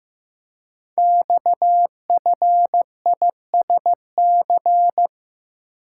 Morse code, heard as a keyed tone, about 15 wpm, 700 Hz. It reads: XFISC
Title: Code Groups - 5 Letters - 15wpm